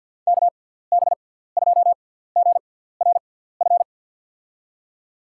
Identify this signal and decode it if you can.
Morse code, heard as a keyed tone, about 52 wpm, spaced as for 17 wpm, 700 Hz. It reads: KB2GRF
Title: USA Call Signs - 6 Characters - 6x Character Spacing - 50wpm